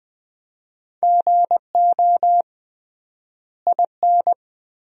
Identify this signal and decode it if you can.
Morse code, heard as a keyed tone, about 21 wpm, 700 Hz. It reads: GO IN
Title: 3x Word-Spacing - Sentences from Top 500 Words - 20wpm